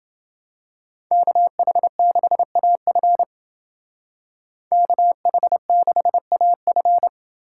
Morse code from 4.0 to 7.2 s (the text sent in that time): KH6AF